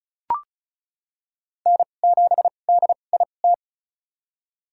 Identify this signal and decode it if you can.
Morse code, heard as a keyed tone, about 36 wpm, spaced as for 25 wpm, 700 Hz. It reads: N7DIT
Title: USA Call Signs - 5 Characters - 2x Character Spacing - 35wpm